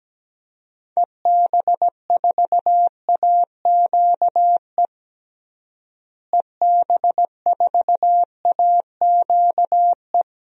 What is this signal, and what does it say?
Morse code, heard as a keyed tone, about 17 wpm, 700 Hz. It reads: EB4AQE EB4AQE
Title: DXCC-100 Call Signs - 6 Characters - Repeated 2x - 17wpm